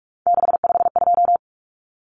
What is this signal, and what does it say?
Morse code, heard as a keyed tone, about 44 wpm, 700 Hz. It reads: 652
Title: Common Cut Numbers - 3-Digit 45wpm